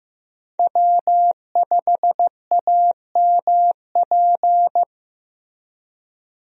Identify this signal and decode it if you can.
Morse code, heard as a keyed tone, about 15 wpm, 700 Hz. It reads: W5AMP
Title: USA Call Signs - 5 Characters - Repeated 2x - 15wpm